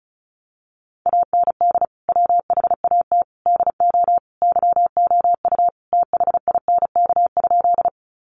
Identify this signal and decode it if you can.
Morse code, heard as a keyed tone, about 34 wpm, 700 Hz. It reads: AND WHAT DO YOU THINK?
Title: Sentences from Top 200 Words 35wpm